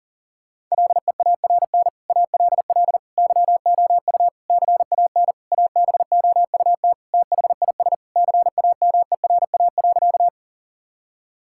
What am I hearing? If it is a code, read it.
Morse code, heard as a keyed tone, about 40 wpm, 700 Hz. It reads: LEARN ALL YOU CAN ABOUT THIS CAMERA.
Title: Sentences from Top 1000 Words - Review Part 5 - 40wpm